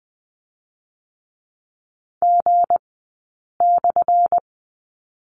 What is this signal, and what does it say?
Morse code, heard as a keyed tone, about 19 wpm, spaced as for 8 wpm, 700 Hz. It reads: G/
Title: 183 Lesson 30d - Letter b - Two Characters 20wpm